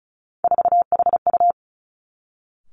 Morse code, heard as a keyed tone, about 36 wpm, 700 Hz. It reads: 4HU
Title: Three Challenging Letter-Number Combinations 35wpm